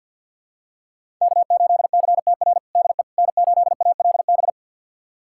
Morse code, heard as a keyed tone, about 51 wpm, 700 Hz. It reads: K8CTR DE N8ALB
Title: USA Callsign DE Callsign - 5 Characters - 50wpm